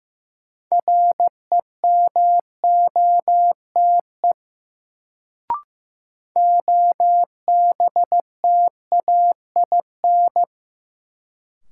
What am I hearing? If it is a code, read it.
Morse code, heard as a keyed tone, about 15 wpm, 700 Hz. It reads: REMOTE OBTAIN
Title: Sentences from Top 1300 Words - New Word Drill Alt - 15wpm